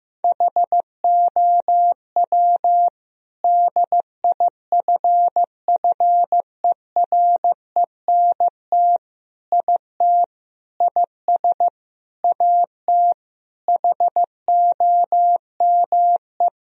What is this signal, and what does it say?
Morse code, heard as a keyed tone, about 16 wpm, 700 Hz. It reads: HOW DIFFERENT IT IS AT HOME
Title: Sentences from Top 1000 Words - Review Part 3 - 15wpm